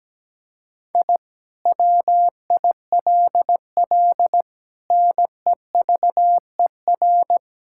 Morse code, heard as a keyed tone, about 17 wpm, 700 Hz. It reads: I WILL NEVER
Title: Sentences from Top 1000 Words - Review Part 6 - 17wpm